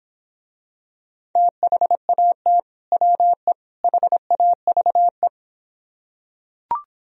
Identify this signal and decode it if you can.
Morse code, heard as a keyed tone, about 24 wpm, 700 Hz. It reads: THAT WE HAVE